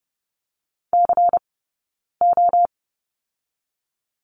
Morse code, heard as a keyed tone, about 30 wpm, 700 Hz. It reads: C O
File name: Lesson 18d - Letter m - Two Characters 30wpm